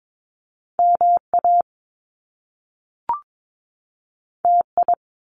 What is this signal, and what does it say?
Morse code, heard as a keyed tone, about 22 wpm, 700 Hz. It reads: MA TI
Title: Top Two Letter Combinations in Words 22wpm